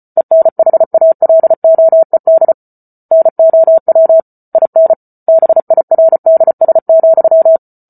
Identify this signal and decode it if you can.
Morse code, heard as a keyed tone, about 34 wpm, 650 Hz. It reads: ENHALOED NOW IN BIRDS,